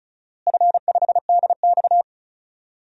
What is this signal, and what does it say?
Morse code, heard as a keyed tone, about 36 wpm, 700 Hz. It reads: F5DX
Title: DXCC-100 Call Signs - 4 Characters - Repeated 3x - 35wpm